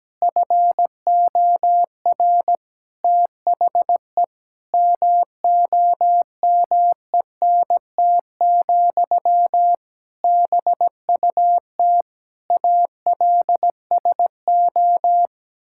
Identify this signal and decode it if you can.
Morse code, heard as a keyed tone, about 17 wpm, 700 Hz. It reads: FOR THE MOMENT, BUT ALSO
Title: Sentences from Top 1000 Words - Review Part 6 - 17wpm